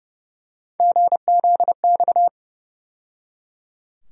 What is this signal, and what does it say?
Morse code, heard as a keyed tone, about 29 wpm, 700 Hz. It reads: GZX
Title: USA Call Sign Postfixes - 3 Letters 30wpm